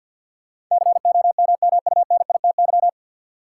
Morse code, heard as a keyed tone, about 52 wpm, 700 Hz. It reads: COMMUNITY
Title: Sentences from Top 1000 Words - Review Part 4 - New Word Drill - 50wpm